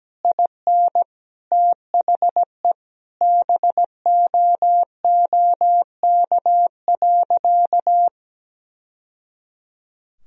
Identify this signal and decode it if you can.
Morse code, heard as a keyed tone, about 17 wpm, 700 Hz. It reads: IN THE BOOK.
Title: Sentences from Top 1000 Words - Review Part 1 - 17wpm